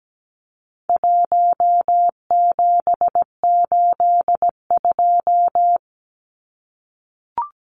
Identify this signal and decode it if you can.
Morse code, heard as a keyed tone, about 17 wpm, 700 Hz. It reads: 1782
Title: Numbers - 4-Digit 17wpm